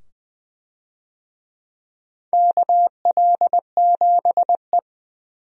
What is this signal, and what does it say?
Morse code, heard as a keyed tone, about 21 wpm, 700 Hz. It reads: KL7E
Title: DXCC-100 Call Signs - 4 Characters - 20wpm